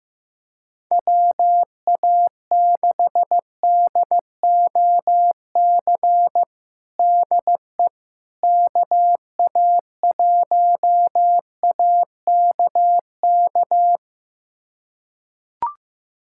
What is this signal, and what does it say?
Morse code, heard as a keyed tone, about 15 wpm, 700 Hz. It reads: WA6DOC DE KA1AKK
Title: USA Callsign DE Callsign - 6 Characters - 15wpm